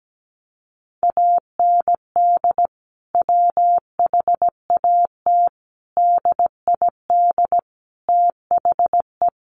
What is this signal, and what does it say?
Morse code, heard as a keyed tone, about 17 wpm, 700 Hz. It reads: AND WHAT DID THE
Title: Sentences from Top 1500 Words 17wpm